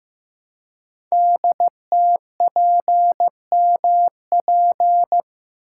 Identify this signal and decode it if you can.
Morse code, heard as a keyed tone, about 15 wpm, 700 Hz. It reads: DTPMP
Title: Code Groups - 5 Letters - 15wpm